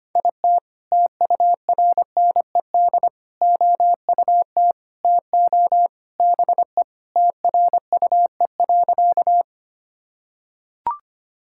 Morse code, heard as a keyed tone, about 25 wpm, 700 Hz. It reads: IT TURNED OUT TO BE TRUE.
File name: practice / 25wpm / Sentences with 3-Word Phrases - 25wpm